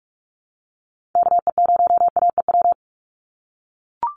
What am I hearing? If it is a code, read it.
Morse code, heard as a keyed tone, about 44 wpm, 700 Hz. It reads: KE0AEW
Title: USA Call Signs 45wpm